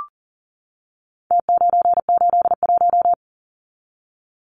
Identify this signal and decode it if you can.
Morse code, heard as a keyed tone, about 40 wpm, 700 Hz. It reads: T981
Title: Numbers - Contest Style - 4-Digit 40wpm